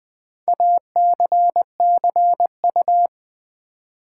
Morse code, heard as a keyed tone, about 20 wpm, 700 Hz. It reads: ACCU
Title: Build A Word - 2 Letters At A Time - 8-Letter Words - 20wpm